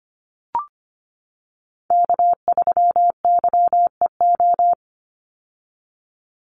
Morse code, heard as a keyed tone, about 25 wpm, 700 Hz. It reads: K3YEO
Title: Call Signs - 25wpm